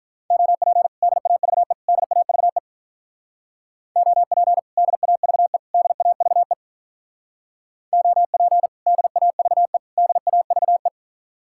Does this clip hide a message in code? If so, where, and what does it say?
Yes: Morse code from 0.3 to 10.9 s, heard as a keyed tone, about 44 wpm, 700 Hz. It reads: OP DAVE DAVE OP DAVE DAVE OP DAVE DAVE